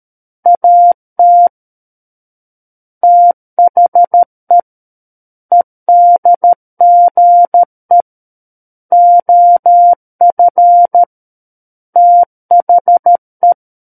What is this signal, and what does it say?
Morse code, heard as a keyed tone, about 13 wpm, 700 Hz. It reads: AT THE EDGE OF THE
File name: Book One / 04-Chapter 4 - 13 WPM